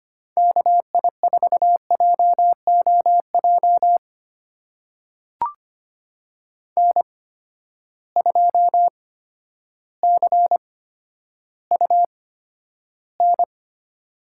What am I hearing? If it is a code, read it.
Morse code, heard as a keyed tone, about 25 wpm, spaced as for 11 wpm, 700 Hz. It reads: KI4JOJ N2CUN
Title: USA Call Signs - 8x Character Spacing - 25wpm